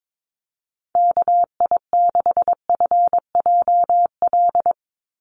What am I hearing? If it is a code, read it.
Morse code, heard as a keyed tone, about 22 wpm, 700 Hz. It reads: KI6FJL
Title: USA Call Signs 22wpm